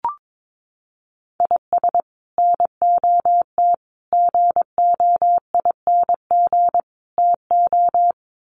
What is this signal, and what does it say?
Morse code, heard as a keyed tone, about 22 wpm, 700 Hz. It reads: IS NOT GOING TO
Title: Sets of 4 Words 22wpm